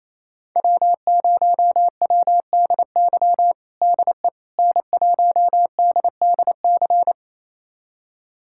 Morse code, heard as a keyed tone, about 28 wpm, 700 Hz. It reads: W0WDY DE N1DDC
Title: USA Callsign DE Callsign - 5 Characters - 28wpm